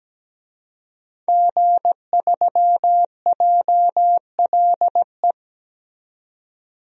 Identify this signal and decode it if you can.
Morse code, heard as a keyed tone, about 17 wpm, 700 Hz. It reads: G3JLE
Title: DXCC-100 Call Signs - 5 Characters - 17wpm